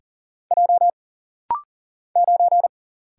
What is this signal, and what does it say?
Morse code, heard as a keyed tone, about 38 wpm, 700 Hz. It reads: J 9
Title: Single Letter-Number - Rapid-Fire 40wpm